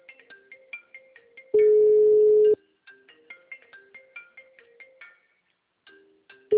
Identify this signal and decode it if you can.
Ringback tone (Europe, 425 Hz).